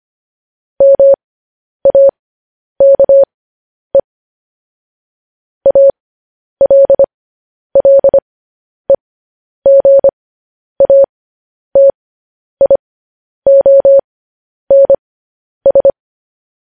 Morse code, heard as a keyed tone, about 25 wpm, spaced as for 10 wpm, 550 Hz. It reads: MAKE ALLEGATIONS
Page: news headlines in morse code at 10 wpm